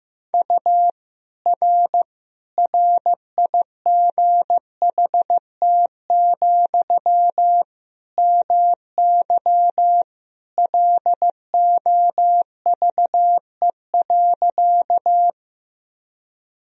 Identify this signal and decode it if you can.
Morse code, heard as a keyed tone, about 15 wpm, 700 Hz. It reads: U R RIGHT, MY LOVE.
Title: Sentences from Top 400 Words and Abbreviations 15wpm